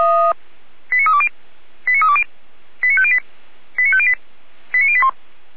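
Five-tone selective call (ZVEI), heard as quick runs of five tones, tones of about 70 ms. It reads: 89320 89320 89508 89508 89081
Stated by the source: ZVEI five-tone selective call received with an SDR.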